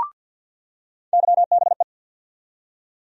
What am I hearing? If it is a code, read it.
Morse code, heard as a keyed tone, about 47 wpm, 700 Hz. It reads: YBE
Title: USA Call Sign Postfixes - 3 Letters 50wpm